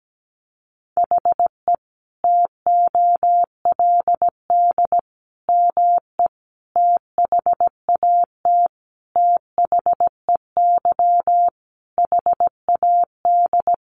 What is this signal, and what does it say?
Morse code, heard as a keyed tone, about 17 wpm, 700 Hz. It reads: HE TOLD ME THAT THEY HAD
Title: Sentences from Top 1300 Words 17wpm